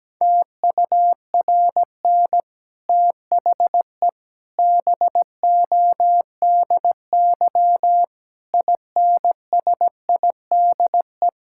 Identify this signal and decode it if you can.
Morse code, heard as a keyed tone, about 17 wpm, 700 Hz. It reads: TURN THE BODY INSIDE
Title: Sentences from Top 1000 Words - Review Part 6 - 17wpm